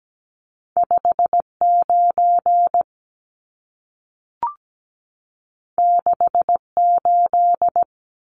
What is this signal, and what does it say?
Morse code, heard as a keyed tone, about 17 wpm, 700 Hz. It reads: 59 68
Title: Numbers - 2-Digit 17wpm